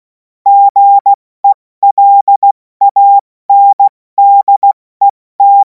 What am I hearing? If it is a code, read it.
Morse code, heard as a keyed tone, about 16 wpm, spaced as for 14 wpm, 800 Hz. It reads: GELANDET
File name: Morsekurs / Witze / Witz 31. März 2026 70 F80